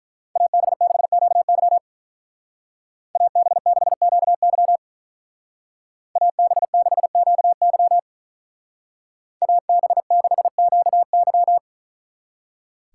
Morse code, heard as a keyed tone, about 44 wpm, 700 Hz. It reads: AB6QY AB6QY AB6QY AB6QY
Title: Speed Racing - USA Call Signs 35wpm